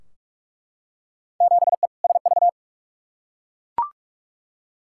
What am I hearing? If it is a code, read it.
Morse code, heard as a keyed tone, about 42 wpm, 700 Hz. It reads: 7E SV
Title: Field Day Exchanges Encore - 45wpm